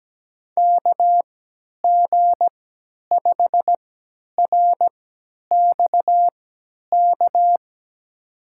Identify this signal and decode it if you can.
Morse code, heard as a keyed tone, about 17 wpm, spaced as for 10 wpm, 700 Hz. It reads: KG5RXK
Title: USA Call Signs - 6 Characters - 3x Character Spacing - 17wpm